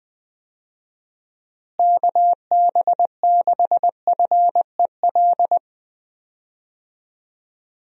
Morse code, heard as a keyed tone, about 21 wpm, 700 Hz. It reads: KB6FEL
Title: USA Call Signs - 6 Characters - Repeated 2x - 20wpm